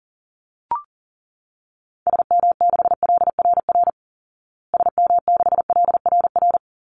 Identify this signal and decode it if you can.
Morse code, heard as a keyed tone, about 40 wpm, 700 Hz. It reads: SM6LRR SM6LRR
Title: Call Signs Repeated 2x - 40wpm